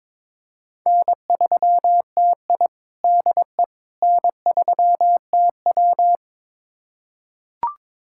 Morse code, heard as a keyed tone, about 23 wpm, 700 Hz. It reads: N3TI DE N3TW
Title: USA Callsign DE Callsign - 4 Characters - 22wpm